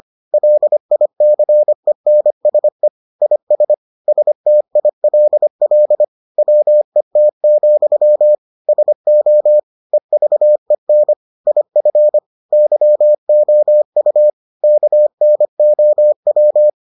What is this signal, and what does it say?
Morse code, heard as a keyed tone, about 25 wpm, 600 Hz. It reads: LICENSE IS STILL WET, SO EVEN IF YOU KNOW